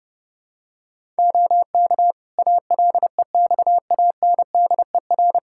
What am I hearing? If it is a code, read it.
Morse code, heard as a keyed tone, about 32 wpm, 700 Hz. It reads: OK ALEXANDER
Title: K1USN SST QSO Practice 30wpm